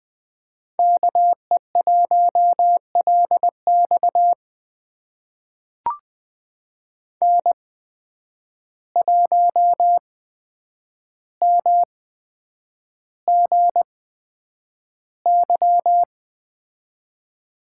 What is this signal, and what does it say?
Morse code, heard as a keyed tone, about 20 wpm, spaced as for 9 wpm, 700 Hz. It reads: KE1LX N1MGY
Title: USA Call Signs - 5 Characters - 8x Character Spacing - 20wpm